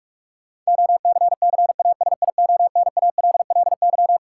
Morse code, heard as a keyed tone, about 45 wpm, 700 Hz. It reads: OCCASIONALLY
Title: Sentences from Top 1300 Words - New Word Drill - 45wpm